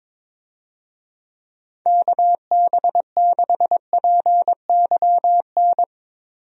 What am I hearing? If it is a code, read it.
Morse code, heard as a keyed tone, about 22 wpm, 700 Hz. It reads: KB6PYN
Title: USA Call Signs - 6 Characters - Repeated 2x - 22wpm